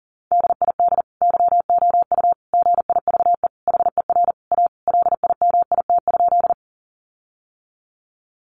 Morse code, heard as a keyed tone, about 40 wpm, 700 Hz. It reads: DID YOU GIVE HER A LIMIT?
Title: Sentences from Top 1500 Words 40wpm